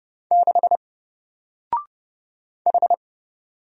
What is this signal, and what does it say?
Morse code, heard as a keyed tone, about 30 wpm, 700 Hz. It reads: 6 H
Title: Single Letter-Number Alt 30wpm